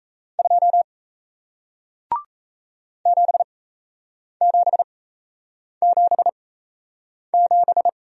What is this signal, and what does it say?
Morse code, heard as a keyed tone, about 37 wpm, spaced as for 8 wpm, 700 Hz. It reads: 2 7777